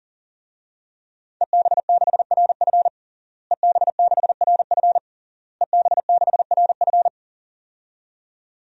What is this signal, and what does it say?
Morse code, heard as a keyed tone, about 40 wpm, 700 Hz. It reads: EB6RF EB6RF EB6RF
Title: DXCC-100 Call Signs - 5 Characters - Repeated 3x - 40wpm